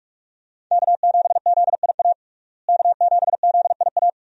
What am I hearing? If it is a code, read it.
Morse code, heard as a keyed tone, about 46 wpm, 700 Hz. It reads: K7ZIA K7ZIA
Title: USA Call Signs - 5 Characters - Repeated 2x - 45wpm